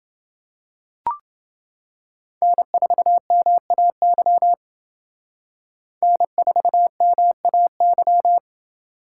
Morse code, heard as a keyed tone, about 29 wpm, 700 Hz. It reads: N4MAY N4MAY